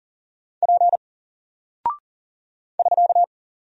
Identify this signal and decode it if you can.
Morse code, heard as a keyed tone, about 39 wpm, 700 Hz. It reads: P <SK>
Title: Single Character (Letter:Number:Punctuation:Prosign) Alt 40wpm